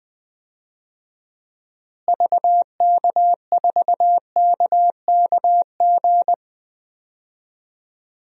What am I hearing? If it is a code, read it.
Morse code, heard as a keyed tone, about 20 wpm, 700 Hz. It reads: VK4KKG
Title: DXCC-100 Call Signs - 6 Characters - Repeated 2x - 20wpm